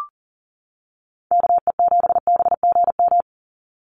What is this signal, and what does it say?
Morse code, heard as a keyed tone, about 40 wpm, 700 Hz. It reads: KE7BGM
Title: NA SOTA Activator Callsigns 40wpm